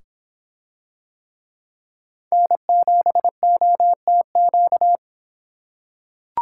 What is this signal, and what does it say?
Morse code, heard as a keyed tone, about 26 wpm, 700 Hz. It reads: N7OTQ